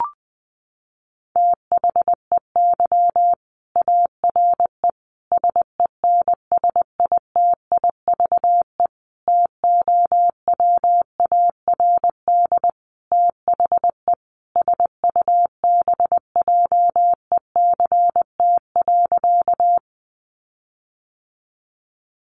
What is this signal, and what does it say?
Morse code, heard as a keyed tone, about 19 wpm, 700 Hz. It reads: THEY ARE SENSITIVE TOWARD THE SUBJECT.
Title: Sentences from Top 1400 Words 20wpm